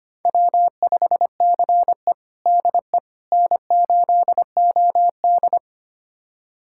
Morse code, heard as a keyed tone, about 25 wpm, 700 Hz. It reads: W5CE DE N8OD
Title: USA Callsign DE Callsign - 4 Characters - 25wpm